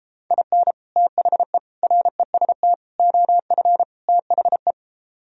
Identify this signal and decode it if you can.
Morse code, heard as a keyed tone, about 33 wpm, 700 Hz. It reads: IN THE REST OF THE